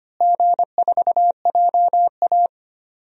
Morse code, heard as a keyed tone, about 25 wpm, 700 Hz. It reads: G4JA
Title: DXCC-100 Call Signs - 4 Characters - Repeated 3x - 25wpm